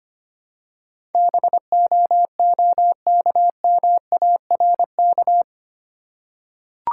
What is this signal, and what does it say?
Morse code, heard as a keyed tone, about 25 wpm, 690 Hz. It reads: BOOKMARK
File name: practice / 25wpm / Easy Words with Hints - 25wpm